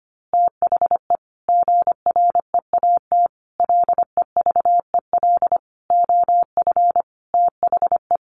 Morse code, heard as a keyed tone, about 25 wpm, 700 Hz. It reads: THE GREAT LEVEL OF THE
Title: A Sound Of Thunder 25wpm